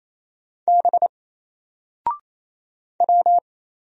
Morse code, heard as a keyed tone, about 27 wpm, 700 Hz. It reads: B W
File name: Single-Letter-Number-28wpm